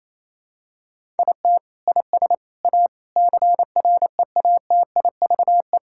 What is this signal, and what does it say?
Morse code, heard as a keyed tone, about 28 wpm, 700 Hz. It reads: IT IS A CREATIVE
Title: Sentences from Top 1000 Words - Review Part 5 - 28wpm